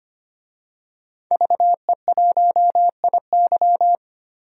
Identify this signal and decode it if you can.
Morse code, heard as a keyed tone, about 25 wpm, 700 Hz. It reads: VE1IY
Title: DXCC-100 Call Signs - 5 Characters - 25wpm